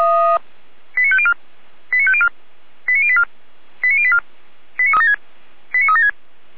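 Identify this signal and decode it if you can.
Five-tone selective call (ZVEI), heard as quick runs of five tones, tones of about 70 ms. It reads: 89504 89504 89084 89084 89376
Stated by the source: ZVEI five-tone selective call received by SDR.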